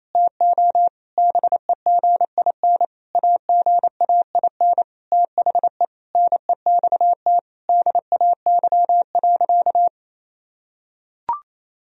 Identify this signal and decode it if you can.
Morse code, heard as a keyed tone, about 28 wpm, 700 Hz. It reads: TO BEGIN AGAIN THE NEXT DAY.
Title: Sentences from Top 1000 Words - Review Part 1 - 28wpm